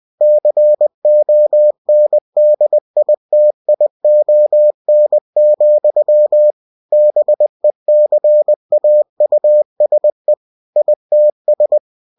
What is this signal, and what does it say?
Morse code, heard as a keyed tone, about 20 wpm, 600 Hz. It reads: CONDITION, BECAUSE ITS
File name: cw-books / winnie-the-pooh / winnie-20wpm-page-0048